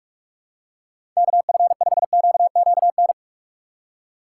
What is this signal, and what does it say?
Morse code, heard as a keyed tone, about 46 wpm, 700 Hz. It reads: KF5QQN
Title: USA Call Signs - 6 Characters - 4x Character Spacing - 45wpm